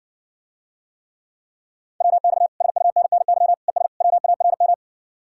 Morse code, heard as a keyed tone, about 60 wpm, 700 Hz. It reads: WX SUNNY ES WARM